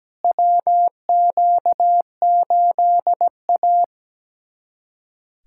Morse code, heard as a keyed tone, about 17 wpm, 700 Hz. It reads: WQ8A
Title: USA Callsign DE Callsign - 4 Characters - 17wpm